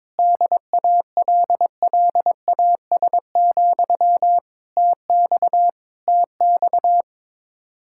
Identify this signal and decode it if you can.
Morse code, heard as a keyed tone, about 21 wpm, 700 Hz. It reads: DALLAS, TX TX
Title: Practice QSO - Exchange 1 - Beginning - 22wpm